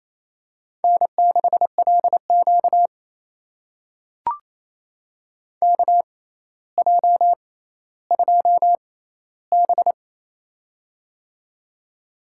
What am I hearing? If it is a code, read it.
Morse code, heard as a keyed tone, about 28 wpm, spaced as for 14 wpm, 700 Hz. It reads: N6LQ KJ2B
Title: USA Call Signs - 4 Characters - 6x Character Spacing - 28wpm